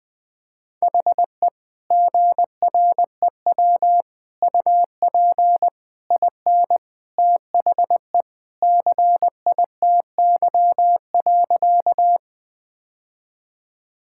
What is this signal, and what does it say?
Morse code, heard as a keyed tone, about 21 wpm, 700 Hz. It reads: HE GREW UP IN THE CITY.
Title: Sentences with 3-Word Phrases - 20wpm